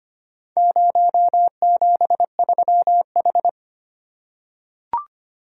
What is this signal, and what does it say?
Morse code, heard as a keyed tone, about 25 wpm, 700 Hz. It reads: 073H
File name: practice / 25wpm / Four Letter-Number Combinations - 25wpm